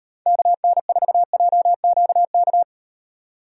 Morse code, heard as a keyed tone, about 38 wpm, 700 Hz. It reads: KN4JQK